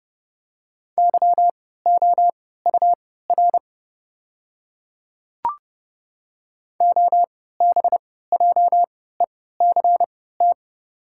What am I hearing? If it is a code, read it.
Morse code, heard as a keyed tone, about 30 wpm, spaced as for 17 wpm, 690 Hz. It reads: YOUR OBJECT
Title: 3x Character Spacing - Top 500 Words - 30wpm